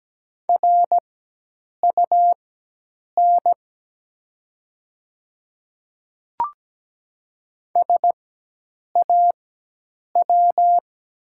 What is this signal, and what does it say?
Morse code, heard as a keyed tone, about 17 wpm, spaced as for 8 wpm, 700 Hz. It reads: RUN SAW
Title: 4x Character Spacing - Top 300 Words - 17wpm